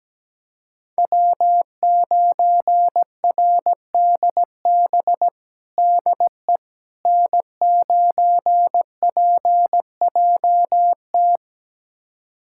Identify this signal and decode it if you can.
Morse code, heard as a keyed tone, about 17 wpm, 700 Hz. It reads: W9RDB DE N9PJT
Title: USA Callsign DE Callsign - 5 Characters - 17wpm